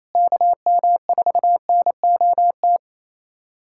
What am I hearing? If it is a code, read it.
Morse code, heard as a keyed tone, about 28 wpm, 700 Hz. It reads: KM4NOT
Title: USA Callsign DE Callsign - 6 Characters - 28wpm